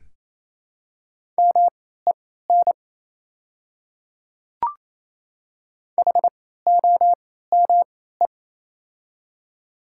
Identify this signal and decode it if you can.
Morse code, heard as a keyed tone, about 28 wpm, spaced as for 16 wpm, 700 Hz. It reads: MEN HOME
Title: 3x Character Spacing - Top 300 Words - 28wpm